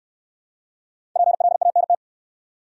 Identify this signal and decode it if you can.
Morse code, heard as a keyed tone, about 66 wpm, 690 Hz. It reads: PLANT